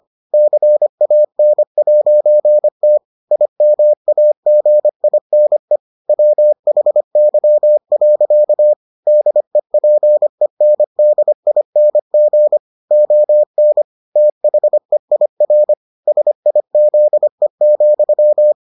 Morse code, heard as a keyed tone, about 25 wpm, 600 Hz. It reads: CAN'T IMAGINE WHY. DEPENDING ON THEIR SIZE,